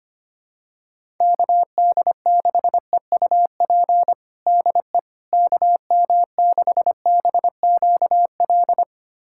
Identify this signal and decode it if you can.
Morse code, heard as a keyed tone, about 25 wpm, 700 Hz. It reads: KD6EUP DE KM6BQL
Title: USA Callsign DE Callsign - 6 Characters - 25wpm